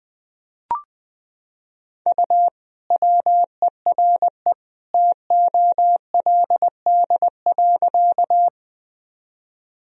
Morse code, heard as a keyed tone, about 20 wpm, 700 Hz. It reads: U WERE TOLD.
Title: Sentences from Top 400 Words and Abbreviations 20wpm